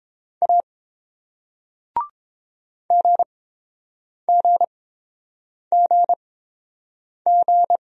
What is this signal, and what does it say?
Morse code, heard as a keyed tone, about 30 wpm, spaced as for 7 wpm, 700 Hz. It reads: A GGGG